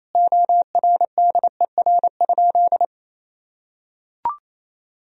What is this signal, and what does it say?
Morse code, heard as a keyed tone, about 28 wpm, 700 Hz. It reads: ORDER?
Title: Sentences from Top 400 Words and Abbreviations 28wpm